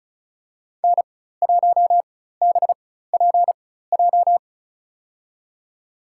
Morse code, heard as a keyed tone, about 36 wpm, spaced as for 16 wpm, 700 Hz. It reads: N1BPJ
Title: USA Call Signs - 5 Characters - 4x Character Spacing - 35wpm